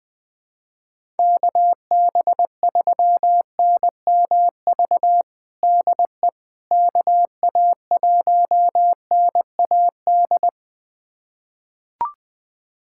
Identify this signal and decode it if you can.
Morse code, heard as a keyed tone, about 20 wpm, 700 Hz. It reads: KB3NMV DE KA1NAD
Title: USA Callsign DE Callsign - 6 Characters - 20wpm